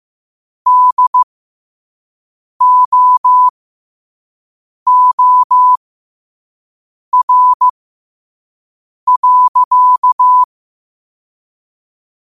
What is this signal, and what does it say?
Morse code, heard as a keyed tone, about 15 wpm, spaced as for 5 wpm, 1000 Hz. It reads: DOOR.
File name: qotdfiles / 5wpm